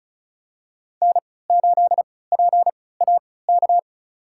Morse code, heard as a keyed tone, about 34 wpm, spaced as for 20 wpm, 700 Hz. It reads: N8PAK
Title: USA Call Signs - 5 Characters - 3x Character Spacing - 35wpm